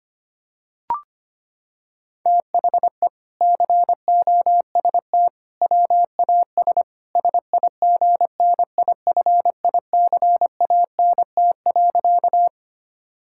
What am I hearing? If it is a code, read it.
Morse code, heard as a keyed tone, about 25 wpm, 700 Hz. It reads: THE COST WAS SIGNIFICANT.
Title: Sentences from Top 1000 Words - Review Part 6 - 25wpm